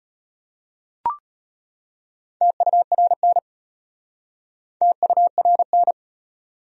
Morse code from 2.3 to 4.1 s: TURN

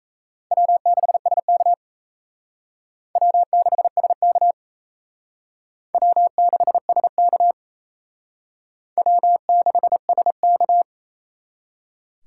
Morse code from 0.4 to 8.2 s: W6SK W6SK W6SK

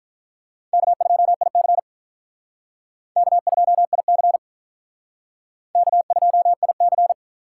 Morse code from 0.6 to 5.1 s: K2IC K2IC

